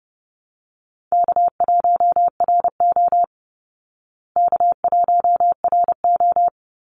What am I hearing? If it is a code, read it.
Morse code, heard as a keyed tone, about 30 wpm, 700 Hz. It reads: K1RO K1RO
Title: Call Signs Repeated 3x - 30wpm